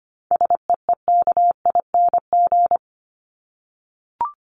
Morse code, heard as a keyed tone, about 25 wpm, 700 Hz. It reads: SEEKING
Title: ING Suffix - Top 100 Words with Base Words 25wpm